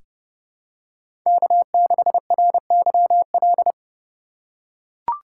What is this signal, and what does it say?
Morse code, heard as a keyed tone, about 31 wpm, 700 Hz. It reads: K6RYL
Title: USA Call Signs - 5 Characters - 3x Character Spacing - 30wpm